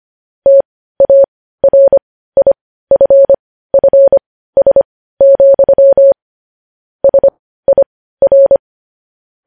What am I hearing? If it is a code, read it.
Morse code, heard as a keyed tone, about 25 wpm, spaced as for 15 wpm, 550 Hz. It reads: TARIFFS, SIR
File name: podcast / News headlines-1x-25x15